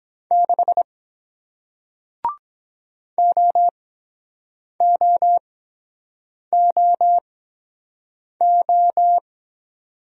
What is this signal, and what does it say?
Morse code, heard as a keyed tone, about 23 wpm, spaced as for 7 wpm, 700 Hz. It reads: 6 OOOO